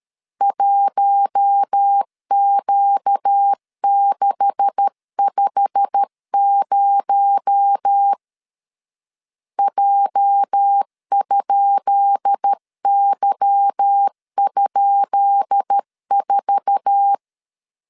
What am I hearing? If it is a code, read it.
Morse code, heard as a keyed tone, about 13 wpm, 800 Hz. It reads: 1Q650 J?Y?4
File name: LEZIONI / 10 SIGNS / 10B INT 12wpm 2414